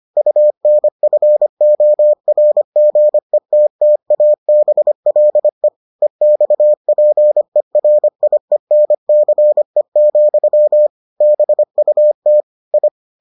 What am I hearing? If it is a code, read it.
Morse code, heard as a keyed tone, about 25 wpm, 600 Hz. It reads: UNFORGETTABLE EXPERIENCE, BUT I